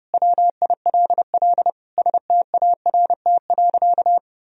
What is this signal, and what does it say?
Morse code, heard as a keyed tone, about 30 wpm, 700 Hz. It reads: WILL START.
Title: Sentences from Top 300 Words and Abbreviations 30wpm